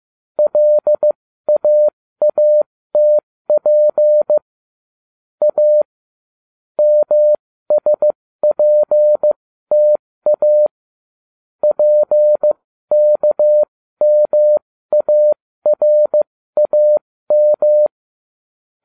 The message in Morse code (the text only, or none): LAATP A MSPTA PKMARAM